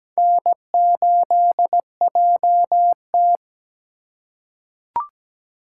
Morse code, heard as a keyed tone, about 17 wpm, 700 Hz. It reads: N8JT
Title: USA Call Signs - 4 Characters - 4x Character Spacing - 17wpm